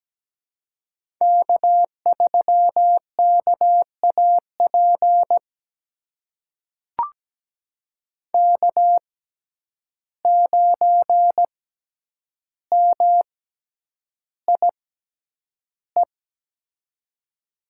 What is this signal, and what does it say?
Morse code, heard as a keyed tone, about 17 wpm, spaced as for 9 wpm, 700 Hz. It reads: K3KAP K9MIE
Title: USA Call Signs - 5 Characters - 6x Character Spacing - 17wpm